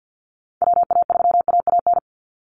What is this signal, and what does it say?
Morse code, heard as a keyed tone, about 49 wpm, 700 Hz. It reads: WA3AAN
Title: Call Signs Repeated 3x - 50wpm